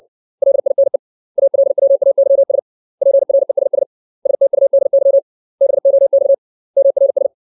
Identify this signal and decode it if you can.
Morse code, heard as a keyed tone, about 60 wpm, 550 Hz. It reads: LINE ALWAYS PRESS STUDY BOX MRS
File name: 632 words / 632 Words 60WPM